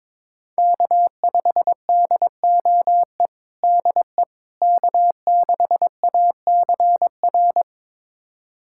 Morse code, heard as a keyed tone, about 22 wpm, 700 Hz. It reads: K5DOE DE K6ACR
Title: USA Callsign DE Callsign - 5 Characters - 22wpm